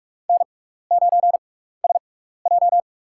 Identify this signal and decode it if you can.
Morse code, heard as a keyed tone, about 46 wpm, spaced as for 15 wpm, 700 Hz. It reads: N9SJ
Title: USA Call Signs - 4 Characters - 6x Character Spacing - 45wpm